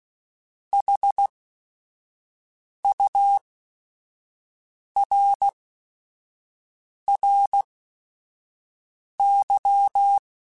Morse code, heard as a keyed tone, about 16 wpm, spaced as for 5 wpm, 800 Hz. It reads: HURRY
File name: Technical Subjects / Technical Subjects-mp3s / AND Gate @5